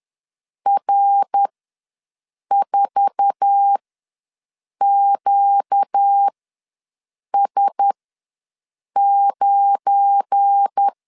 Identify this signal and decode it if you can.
Morse code, heard as a keyed tone, about 10 wpm, spaced as for 6 wpm, 800 Hz. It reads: R4QS9
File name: LEZIONI / 8 YJ45 / 8 YJ4 ws 10wpm 4020